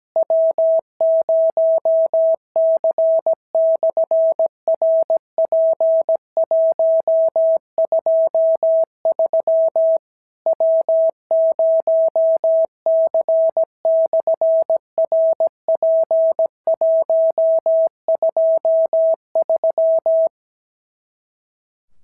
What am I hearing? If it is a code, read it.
Morse code, heard as a keyed tone, about 17 wpm, 650 Hz. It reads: W0C/RP123 W0C/RP123
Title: SOTA QSO Practice 17wpm